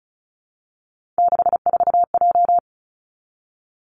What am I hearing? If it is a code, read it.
Morse code, heard as a keyed tone, about 36 wpm, 700 Hz. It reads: 64J
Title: Three Challenging Letter-Number Combinations 35wpm